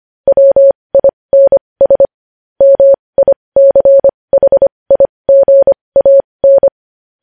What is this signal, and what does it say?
Morse code, heard as a keyed tone, about 25 wpm, spaced as for 20 wpm, 550 Hz. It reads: WINS MICHIGAN
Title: News Headlines at 20